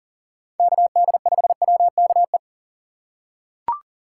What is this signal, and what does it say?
Morse code, heard as a keyed tone, about 40 wpm, 700 Hz. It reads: KD5WKE
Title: USA Call Signs - 6 Characters - 3x Character Spacing - 40wpm